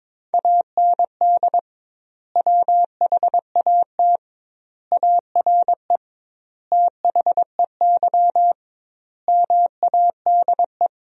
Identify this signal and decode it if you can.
Morse code, heard as a keyed tone, about 22 wpm, 700 Hz. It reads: AND WHAT ARE THEY MADE
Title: 2x Word-Spacing - Sentences from Top 200 Words - 22wpm